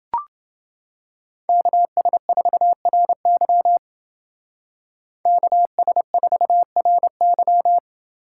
Morse code, heard as a keyed tone, about 28 wpm, 700 Hz. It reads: KS4RY KS4RY